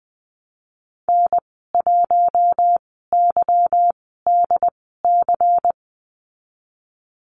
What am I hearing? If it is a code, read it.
Morse code, heard as a keyed tone, about 20 wpm, spaced as for 15 wpm, 700 Hz. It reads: N1YDC
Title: 179 Lesson 29g - Slash - Call Signs 20wpm